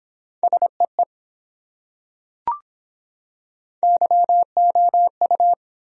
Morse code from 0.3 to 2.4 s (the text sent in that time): SEE